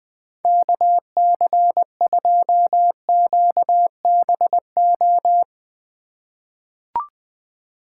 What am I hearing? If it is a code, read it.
Morse code, heard as a keyed tone, about 20 wpm, 700 Hz. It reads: KC2QBO
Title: USA Call Signs - 6 Characters - 2x Character Spacing - 20wpm